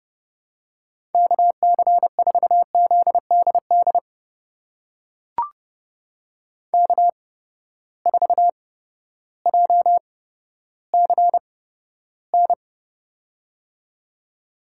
Morse code, heard as a keyed tone, about 30 wpm, spaced as for 14 wpm, 700 Hz. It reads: KC4ZDD K4JCN
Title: USA Call Signs - 8x Character Spacing - 30wpm